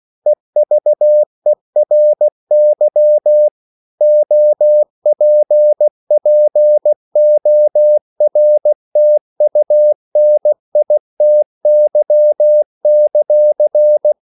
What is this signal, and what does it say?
Morse code, heard as a keyed tone, about 16 wpm, 600 Hz. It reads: EVERY OPPORTUNITY;
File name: CW / Practice / EnglishQuotations0416